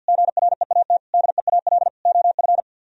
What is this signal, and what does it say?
Morse code, heard as a keyed tone, about 64 wpm, spaced as for 56 wpm, 700 Hz. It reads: GREAT DEAL OF
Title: Speed-Racing - Sets of 4 Words 50wpm